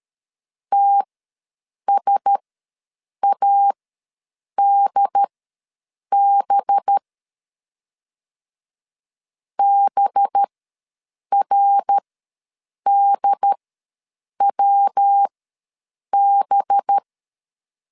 Morse code, heard as a keyed tone, about 13 wpm, spaced as for 7 wpm, 800 Hz. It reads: TSADB BRDWB
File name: 5 LB ws 12wpm